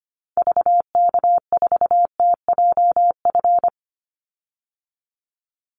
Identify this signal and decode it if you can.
Morse code, heard as a keyed tone, about 25 wpm, 700 Hz. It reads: VK4TJF
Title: Call Signs Repeated 3x - 25wpm